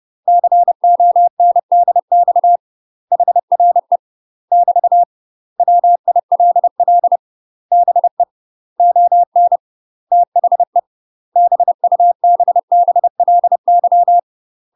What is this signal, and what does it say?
Morse code, heard as a keyed tone, about 30 wpm, 700 Hz. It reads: CONDX HRE <BT> WILL BE ON THE BUBBLY
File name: UK MASTER QSOs TRAINING / 30WPM / UK MASTER QSOs 0018